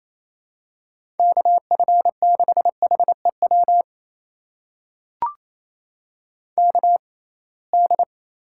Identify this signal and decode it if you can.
Morse code, heard as a keyed tone, about 28 wpm, spaced as for 21 wpm, 700 Hz. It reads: KF6HEW KD
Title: USA Call Signs - 6 Characters - 6x Character Spacing - 28wpm